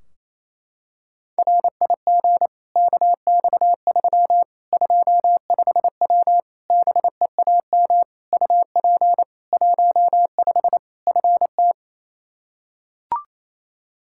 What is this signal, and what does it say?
Morse code, heard as a keyed tone, about 28 wpm, 700 Hz. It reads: RIG KX3 25W BEAM UP 15 FT
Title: K1USN SST QSO Practice 28wpm